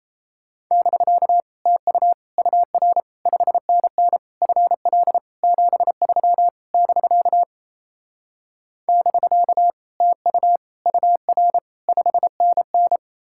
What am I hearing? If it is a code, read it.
Morse code, heard as a keyed tone, about 37 wpm, 700 Hz. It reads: <BK> TU UR 5NN FL 73 <BK> <BK> TU UR 5NN